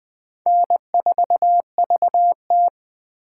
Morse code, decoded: N4VT

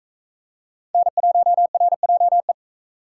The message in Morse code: N1RJE

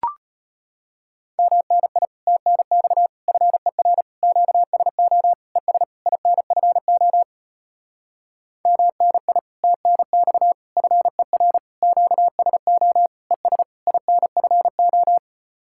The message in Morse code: MNI TNX FER QSO ES INFO MNI TNX FER QSO ES INFO

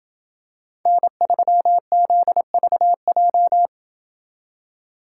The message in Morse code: N3ZVJ

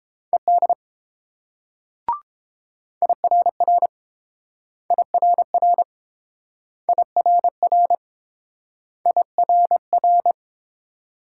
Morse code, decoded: ED IRR IRR IRR IRR

